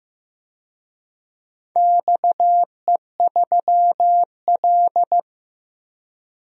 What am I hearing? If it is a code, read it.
Morse code, heard as a keyed tone, about 15 wpm, 700 Hz. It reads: XE3L
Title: DXCC-100 Call Signs - 4 Characters - 15wpm